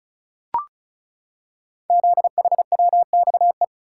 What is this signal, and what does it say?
Morse code, heard as a keyed tone, about 36 wpm, 690 Hz. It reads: ZHWXE
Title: Code Groups - 5 Letters - 35wpm